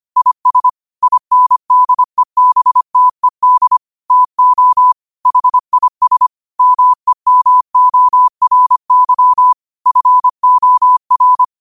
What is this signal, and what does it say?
Morse code, heard as a keyed tone, about 25 wpm, 1000 Hz. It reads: IS INDEBTED TO HIS MEMORY FOR
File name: 25wpm